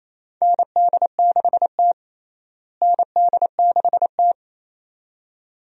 Morse code, heard as a keyed tone, about 28 wpm, 700 Hz. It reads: ND6T ND6T
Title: USA Call Signs - 4 Characters - Repeated 2x - 28wpm